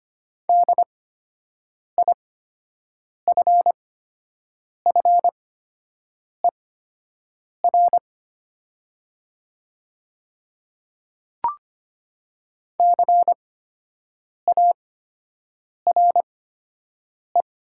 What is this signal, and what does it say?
Morse code, heard as a keyed tone, about 25 wpm, spaced as for 7 wpm, 700 Hz. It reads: DIFFER CARE
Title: 8x Character Spacing - Top 300 Words - 25wpm